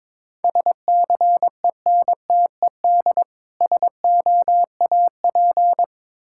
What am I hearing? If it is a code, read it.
Morse code, heard as a keyed tone, about 22 wpm, 700 Hz. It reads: SCENTED SOAP
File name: Difficult Words with Hints - 22wpm